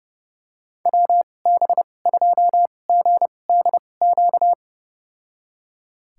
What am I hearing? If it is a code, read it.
Morse code, heard as a keyed tone, about 30 wpm, spaced as for 22 wpm, 700 Hz. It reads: WB2GDQ
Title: USA Call Signs - 2x Character Spacing - 30wpm